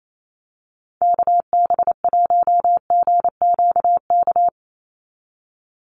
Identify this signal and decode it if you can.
Morse code, heard as a keyed tone, about 28 wpm, 700 Hz. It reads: KB1GQK
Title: USA Call Signs 28wpm